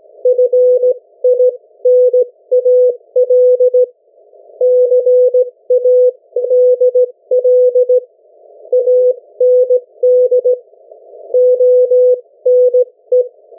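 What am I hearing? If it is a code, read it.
Morse code, heard as a keyed tone, about 17 wpm, spaced as for 15 wpm, 500 Hz. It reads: FINAL CADL AND ONE
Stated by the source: Morse code from the national amateur radio CW net of New Zealand, which meets on 3535.0 kHz or 7030 kHz.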